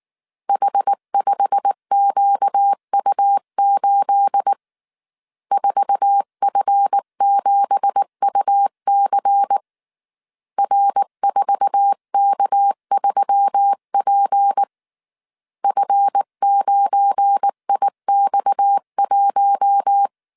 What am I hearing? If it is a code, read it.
Morse code, heard as a keyed tone, about 19 wpm, 800 Hz. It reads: H5QU8 4F7UC R4K3P F9IX1
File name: CW / DET / 1 9 1500C S B 18wpm 2551